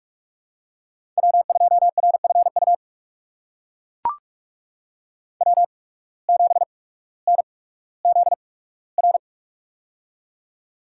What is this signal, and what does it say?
Morse code, heard as a keyed tone, about 41 wpm, spaced as for 19 wpm, 700 Hz. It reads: W2RFU W7NZR
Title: USA Call Signs - 5 Characters - 8x Character Spacing - 45wpm